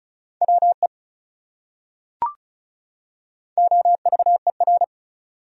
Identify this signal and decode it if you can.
Morse code, heard as a keyed tone, about 35 wpm, 700 Hz. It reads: WE OVER
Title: Top 100 Words 35wpm